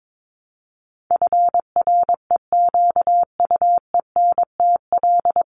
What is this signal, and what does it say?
Morse code, heard as a keyed tone, about 22 wpm, 700 Hz. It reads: FREQUENTL
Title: Top 1001-1250 Words 22wpm